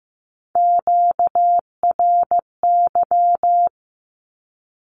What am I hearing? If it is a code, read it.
Morse code, heard as a keyed tone, about 15 wpm, 700 Hz. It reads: QRY
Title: Q-Codes with Spoken Meaning - 15wpm